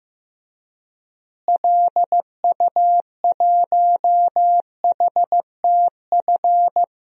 Morse code, heard as a keyed tone, about 15 wpm, 700 Hz. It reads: LU1HTF
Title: DXCC-100 Call Signs - 6 Characters - Repeated 2x - 15wpm